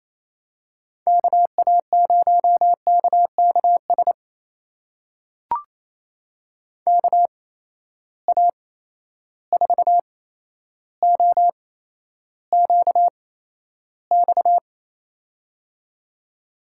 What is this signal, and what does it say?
Morse code, heard as a keyed tone, about 28 wpm, spaced as for 12 wpm, 700 Hz. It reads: KA0KKS KA4OQX
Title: USA Call Signs - 8x Character Spacing - 28wpm